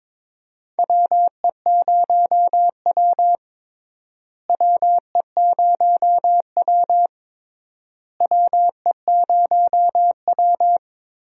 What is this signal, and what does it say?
Morse code, heard as a keyed tone, about 22 wpm, 700 Hz. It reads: WE0W WE0W WE0W
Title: USA Call Signs - 4 Characters - Repeated 3x - 22wpm